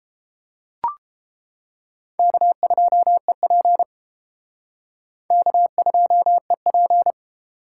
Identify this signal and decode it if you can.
Morse code, heard as a keyed tone, about 30 wpm, 700 Hz. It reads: K2EP K2EP